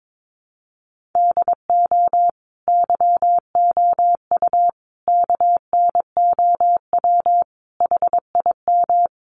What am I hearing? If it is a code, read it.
Morse code, heard as a keyed tone, about 22 wpm, 700 Hz. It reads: DO YOU KNOW HIM
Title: Sentences from Top 1400 Words 22wpm